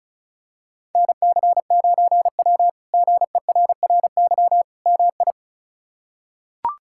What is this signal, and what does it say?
Morse code, heard as a keyed tone, about 36 wpm, 700 Hz. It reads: NC9W GERRY MI
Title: K1USN SST QSO Practice - 35wpm